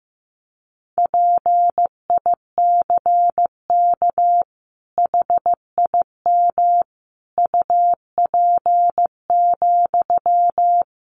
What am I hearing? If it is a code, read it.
Morse code, heard as a keyed tone, about 15 wpm, 700 Hz. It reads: PICK HIM UP,
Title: Sentences from Top 1000 Words 15wpm